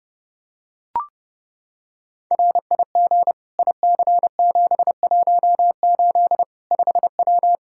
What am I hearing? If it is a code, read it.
Morse code, heard as a keyed tone, about 30 wpm, 700 Hz. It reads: RIG IC718 5W
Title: K1USN SST QSO Practice 30wpm